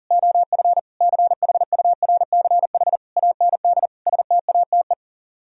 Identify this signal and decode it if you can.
Morse code, heard as a keyed tone, about 38 wpm, 700 Hz. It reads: OF CHURCH AND STATE
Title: Speed-Racing - Sets of 5 Words 40wpm